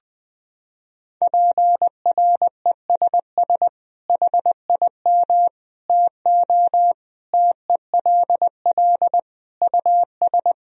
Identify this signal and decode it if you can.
Morse code, heard as a keyed tone, about 20 wpm, 700 Hz. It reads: PRESS HIM TO TELL US
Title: Sentences from Top 1000 Words - Review Part 1 - 20wpm